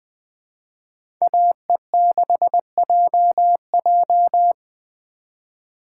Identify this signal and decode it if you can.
Morse code, heard as a keyed tone, about 20 wpm, 700 Hz. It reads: AE6JJ